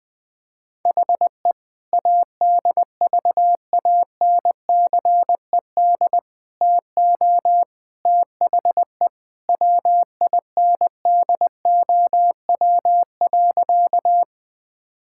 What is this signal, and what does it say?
Morse code, heard as a keyed tone, about 20 wpm, 700 Hz. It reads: HE ADVANCED TO THE WINDOW.
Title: Sentences from Top 1600 Words 20wpm